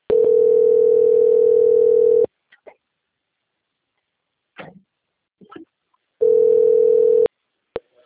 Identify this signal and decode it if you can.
Ringback tone (North America), 440 + 480 Hz.
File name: rg-1600-unknown- - - .35839